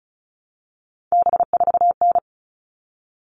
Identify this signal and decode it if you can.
Morse code, heard as a keyed tone, about 34 wpm, 700 Hz. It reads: B4N
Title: All Cut Numbers - 3-Digit 35wpm